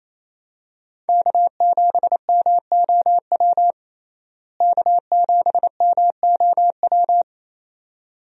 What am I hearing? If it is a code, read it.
Morse code, heard as a keyed tone, about 28 wpm, 700 Hz. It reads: K7MOW K7MOW
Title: USA Call Signs - 5 Characters - Repeated 2x - 28wpm